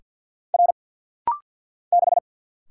Morse code, heard as a keyed tone, about 50 wpm, 700 Hz. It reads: R 6
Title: Single Letter-Number - Rapid-Fire 50wpm